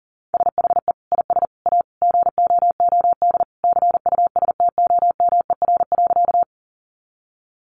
Morse code, heard as a keyed tone, about 40 wpm, 700 Hz. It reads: SHE IS A GOOD CUSTOMER.
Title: Sentences from Top 1100 Words 40wpm